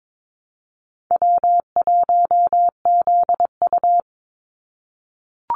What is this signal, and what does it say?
Morse code, heard as a keyed tone, about 22 wpm, 700 Hz. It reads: W1ZU
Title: NA SOTA Activator Callsigns 22wpm